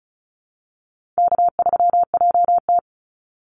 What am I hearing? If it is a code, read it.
Morse code, heard as a keyed tone, about 34 wpm, 700 Hz. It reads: K3JT
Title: Easy Call Signs Repeated 2x - 35wpm